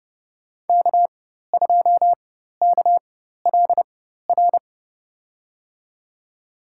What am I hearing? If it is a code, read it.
Morse code, heard as a keyed tone, about 30 wpm, spaced as for 14 wpm, 700 Hz. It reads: K2KLR
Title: USA Call Signs - 4x Character Spacing - 30wpm